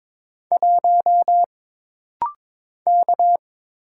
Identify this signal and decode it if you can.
Morse code, heard as a keyed tone, about 22 wpm, 700 Hz. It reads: 1 K
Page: This rapid-fire timing is geared to push your ICR (Instant Character Recognition) skills.